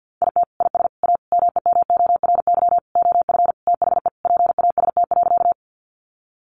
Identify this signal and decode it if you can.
Morse code, heard as a keyed tone, about 50 wpm, 700 Hz. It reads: IT IS A MEMORY OF THE PAST.
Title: Sentences from Top 1300 Words 50wpm